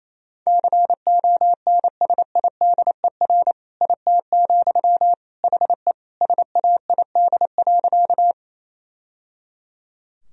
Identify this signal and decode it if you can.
Morse code, heard as a keyed tone, about 28 wpm, 700 Hz. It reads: CONSIDER IT, HE SAID.
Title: Sentences from Top 1000 Words - Review Part 4 - 28wpm